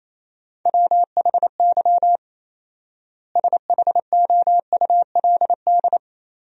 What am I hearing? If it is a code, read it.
Morse code, heard as a keyed tone, about 28 wpm, 700 Hz. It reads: WHY SHOULD
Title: 4x Word-Spacing - Sentences from Top 300 Words - 28wpm